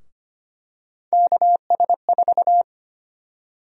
Morse code, heard as a keyed tone, about 25 wpm, 700 Hz. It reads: KS4
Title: USA Call Sign Prefixes - 25wpm